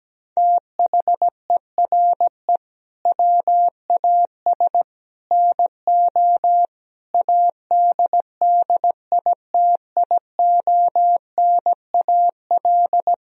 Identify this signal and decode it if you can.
Morse code, heard as a keyed tone, about 17 wpm, 700 Hz. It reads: THERE WAS NO ADDITIONAL
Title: Sentences from Top 1000 Words - Review Part 4 - 17wpm